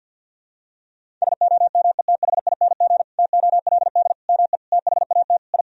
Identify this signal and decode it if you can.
Morse code, heard as a keyed tone, about 52 wpm, 700 Hz. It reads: SOMETHING TOLD ME THAT I
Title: Sentences from Top 1000 Words - Review Part 1 - 50wpm